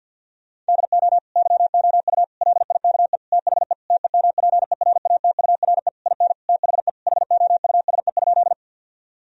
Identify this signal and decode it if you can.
Morse code, heard as a keyed tone, about 50 wpm, 700 Hz. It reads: DO YOU LIKE THE TEMPERATURE IN THE HOUSE?
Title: Sentences from Top 1000 Words - Review Part 4 - 50wpm